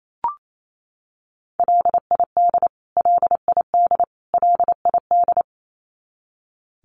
Morse code, heard as a keyed tone, about 28 wpm, 700 Hz. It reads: LID LID LID
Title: Abbreviations Repeated 3x - 28wpm